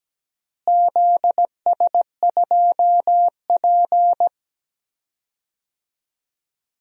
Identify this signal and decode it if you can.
Morse code, heard as a keyed tone, about 17 wpm, 700 Hz. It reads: ZS2P
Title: DXCC-100 Call Signs - 4 Characters - Repeated 2x - 17wpm